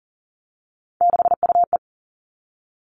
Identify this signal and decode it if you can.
Morse code, heard as a keyed tone, about 40 wpm, 700 Hz. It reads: 6UE
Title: All Cut Numbers - 3-Digit 40wpm